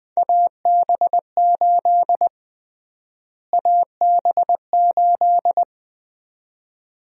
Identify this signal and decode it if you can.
Morse code, heard as a keyed tone, about 20 wpm, 700 Hz. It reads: AB8 AB8
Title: USA Call Sign Prefixes - Repeated 2x - 20wpm